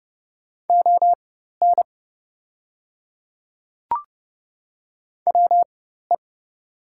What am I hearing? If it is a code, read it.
Morse code, heard as a keyed tone, about 30 wpm, spaced as for 14 wpm, 700 Hz. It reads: ON WE
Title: 4x Character Spacing - Top 100 Words - 30wpm